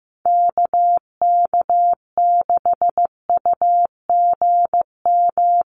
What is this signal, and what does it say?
Morse code, heard as a keyed tone, about 15 wpm, 700 Hz. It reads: KK6UGM
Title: USA Call Signs 15wpm